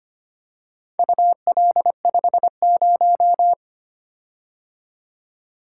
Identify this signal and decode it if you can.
Morse code, heard as a keyed tone, about 25 wpm, 700 Hz. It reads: UL50
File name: practice / 25wpm / Four Letter-Number Combinations - 25wpm